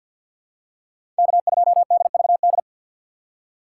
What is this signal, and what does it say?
Morse code, heard as a keyed tone, about 51 wpm, 700 Hz. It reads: K2DVD
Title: USA Call Signs - 5 Characters - 2x Character Spacing - 50wpm